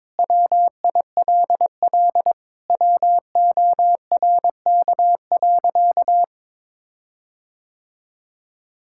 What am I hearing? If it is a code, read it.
Morse code, heard as a keyed tone, about 22 wpm, 700 Hz. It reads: WILL WORK.
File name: Sentences with 7-Word Phrases - 22wpm